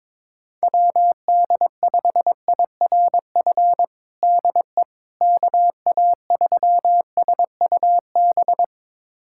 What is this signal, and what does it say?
Morse code, heard as a keyed tone, about 21 wpm, 700 Hz. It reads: WD5IRF DE KA3SUB
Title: USA Callsign DE Callsign - 6 Characters - 22wpm